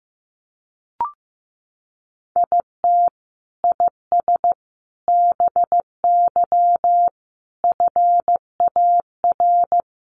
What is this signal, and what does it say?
Morse code, heard as a keyed tone, about 15 wpm, 700 Hz. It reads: IT IS BY FAR
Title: Sentences from Top 1200 Words 15wpm